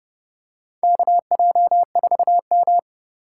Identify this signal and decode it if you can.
Morse code, heard as a keyed tone, about 30 wpm, 700 Hz. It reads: KJ4M
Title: USA Call Signs - 4 Characters - 4x Character Spacing - 30wpm